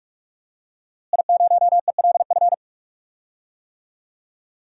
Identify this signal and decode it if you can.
Morse code, heard as a keyed tone, about 46 wpm, 700 Hz. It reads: I0ELF
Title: DXCC-100 Call Signs - 5 Characters - Repeated 2x - 45wpm